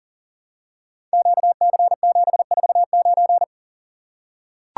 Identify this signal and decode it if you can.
Morse code, heard as a keyed tone, about 42 wpm, 700 Hz. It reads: QC749
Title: Code Groups - 5 Letter:Numbers - 40wpm